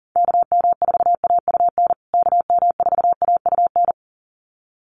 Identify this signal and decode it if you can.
Morse code, heard as a keyed tone, about 40 wpm, 700 Hz. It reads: KM4AUN KM4AUN
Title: USA Call Signs Repeated 2x 40wpm